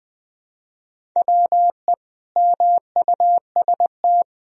Morse code, heard as a keyed tone, about 20 wpm, 700 Hz. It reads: WE MUST
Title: Sentences from Top 400 Words and Abbreviations 20wpm